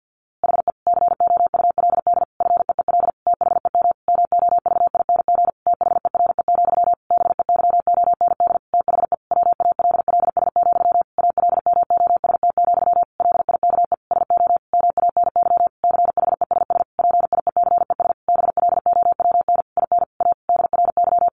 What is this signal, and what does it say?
Morse code, heard as a keyed tone, about 50 wpm, 700 Hz. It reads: HE COULD FEEL THEM MOVING THERE, BEYOND THE WALLS, ALMOST, LIKE SO MANY CHESS PIECES BLOWN IN A DRY
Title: A Sound Of Thunder 50wpm